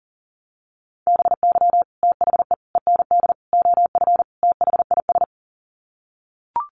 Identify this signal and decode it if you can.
Morse code, heard as a keyed tone, about 40 wpm, 700 Hz. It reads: BY THE END OF THIS
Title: Sets of 5 Words Encore 40wpm